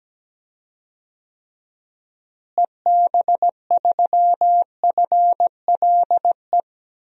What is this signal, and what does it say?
Morse code, heard as a keyed tone, about 17 wpm, 700 Hz. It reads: EB3FLE
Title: DXCC-100 Call Signs - 6 Characters - Repeated 2x - 17wpm